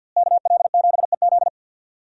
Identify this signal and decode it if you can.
Morse code, heard as a keyed tone, about 46 wpm, 700 Hz. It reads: KL7EZ
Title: DXCC-100 Call Signs - 5 Characters - Repeated 3x - 50wpm